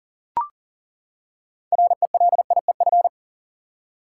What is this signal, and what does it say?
Morse code, heard as a keyed tone, about 40 wpm, 700 Hz. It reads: RELIEF
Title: Six Letter Words - 40wpm